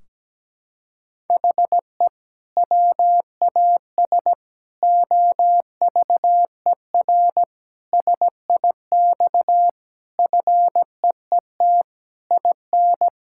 Morse code, heard as a keyed tone, about 17 wpm, 700 Hz. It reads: HE WAS OVER SIX FEET IN
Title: Sentences from Top 1600 Words 17wpm